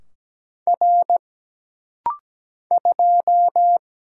Morse code, heard as a keyed tone, about 17 wpm, 700 Hz. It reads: R 2
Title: Single Letter-Number Alt - Rapid-Fire 17wpm